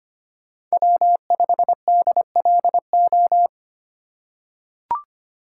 Morse code, heard as a keyed tone, about 25 wpm, 700 Hz. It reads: W5DLO
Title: USA Call Signs - 3x Character Spacing - 25wpm